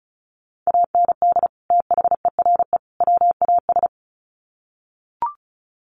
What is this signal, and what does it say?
Morse code, heard as a keyed tone, about 34 wpm, 700 Hz. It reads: AND THERE WAS
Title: Sets of 3 Words 35wpm